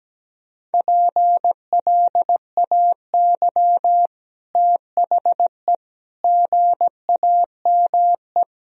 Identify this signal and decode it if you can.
Morse code, heard as a keyed tone, about 17 wpm, 700 Hz. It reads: PLAY THE GAME
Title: Sentences from Top 1000 Words - Review Part 3 - 17wpm